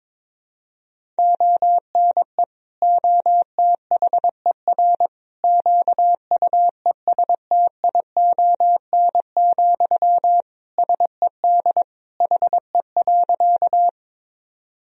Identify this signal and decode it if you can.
Morse code, heard as a keyed tone, about 22 wpm, 700 Hz. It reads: ONE OTHER QUESTION, SED HE.
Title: Sentences from Top 400 Words and Abbreviations 22wpm